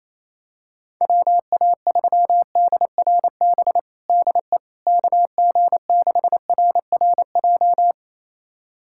Morse code, heard as a keyed tone, about 28 wpm, 700 Hz. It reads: WA3DRB DE KG6RRJ
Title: USA Callsign DE Callsign - 6 Characters - 28wpm